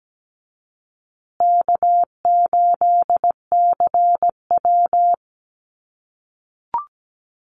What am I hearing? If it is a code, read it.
Morse code, heard as a keyed tone, about 17 wpm, 700 Hz. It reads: K8CW
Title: USA Call Signs Repeated 3x 17wpm